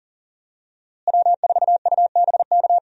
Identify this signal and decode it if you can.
Morse code, heard as a keyed tone, about 40 wpm, 700 Hz. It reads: W4UBK
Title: USA Call Signs - 5 Characters - Repeated 3x - 40wpm